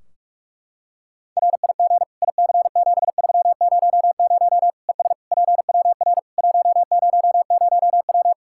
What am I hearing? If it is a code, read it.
Morse code, heard as a keyed tone, about 44 wpm, 700 Hz. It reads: RIG IC7300 ES PWR 100W
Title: Practice QSO - Exchange 2 - Info - 45wpm